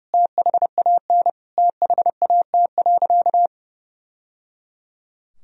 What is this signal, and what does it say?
Morse code, heard as a keyed tone, about 30 wpm, 700 Hz. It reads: THAN THAT.
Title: Sentences from Top 1000 Words - Review Part 2 - 30wpm